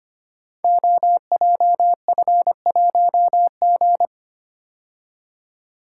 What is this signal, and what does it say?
Morse code, heard as a keyed tone, about 25 wpm, 700 Hz. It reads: OJF1G
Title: Code Groups - 5 Letter:Numbers - 25wpm